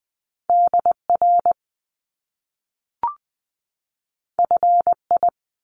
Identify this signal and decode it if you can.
Morse code, heard as a keyed tone, about 20 wpm, 700 Hz. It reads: DR FI
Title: Top Two Letter Combinations in Words 20wpm